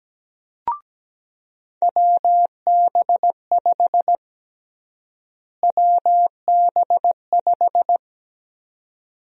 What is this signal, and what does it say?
Morse code, heard as a keyed tone, about 17 wpm, 700 Hz. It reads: WB5 WB5
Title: USA Call Sign Prefixes - Repeated 2x - 17wpm